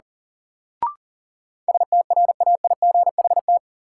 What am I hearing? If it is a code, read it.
Morse code, heard as a keyed tone, about 37 wpm, 700 Hz. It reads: STRAIGHT
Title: Sentences from Top 1000 Words - New Word Drill - 40wpm